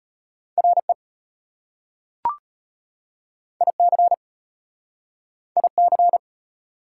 Morse code, heard as a keyed tone, about 33 wpm, 700 Hz. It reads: RE IC IC